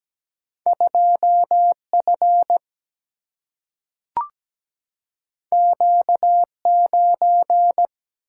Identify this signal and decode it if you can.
Morse code, heard as a keyed tone, about 17 wpm, 700 Hz. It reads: 2F Q9
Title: Single-Letter Single-Number 17wpm